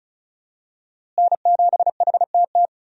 Morse code, heard as a keyed tone, about 36 wpm, 700 Hz. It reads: N7HTT
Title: USA Call Signs - 5 Characters - Repeated 2x - 35wpm